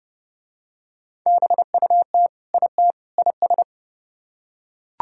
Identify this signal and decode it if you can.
Morse code, heard as a keyed tone, about 29 wpm, 700 Hz. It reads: BUT IT IS